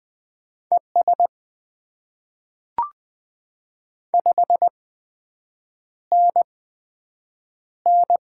Morse code, heard as a keyed tone, about 20 wpm, spaced as for 7 wpm, 700 Hz. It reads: ES 5NN
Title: Top 20 QSO Elements - 8x Character Spacing - 20wpm